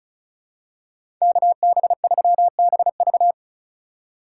Morse code, heard as a keyed tone, about 36 wpm, 700 Hz. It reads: KB3BV
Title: USA Call Signs - 5 Characters - 6x Character Spacing - 35wpm